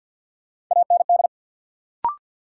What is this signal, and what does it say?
Morse code, heard as a keyed tone, about 53 wpm, 700 Hz. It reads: AND
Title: Sets of 2 Words - New Word Drill - 50wpm